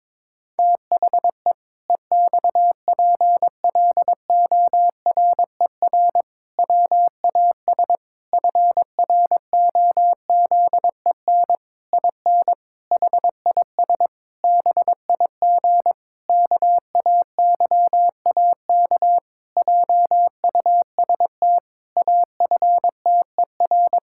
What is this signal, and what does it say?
Morse code, heard as a keyed tone, about 23 wpm, 700 Hz. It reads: THE EXPLORER WAS FROZEN IN HIS BIG KAYAK JUST AFTER
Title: Pangram Challenge v2 - 22wpm